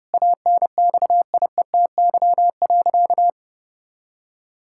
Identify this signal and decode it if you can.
Morse code, heard as a keyed tone, about 30 wpm, 700 Hz. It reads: ANXIETY.
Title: Sentences from Top 1600 Words 30wpm